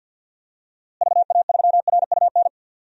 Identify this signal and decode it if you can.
Morse code, heard as a keyed tone, about 49 wpm, 700 Hz. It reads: VA3RUN
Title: DXCC-100 Call Signs - 6 Characters - Repeated 3x - 50wpm